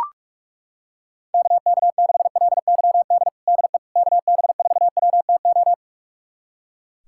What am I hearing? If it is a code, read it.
Morse code, heard as a keyed tone, about 41 wpm, 700 Hz. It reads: KK6LYD DE KB4WTO
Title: USA Callsign DE Callsign - 6 Characters - 45wpm